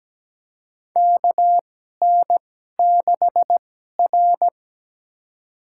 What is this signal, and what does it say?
Morse code, heard as a keyed tone, about 17 wpm, spaced as for 12 wpm, 700 Hz. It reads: KN6R
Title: USA Call Signs - 4 Characters - 2x Character Spacing - 17wpm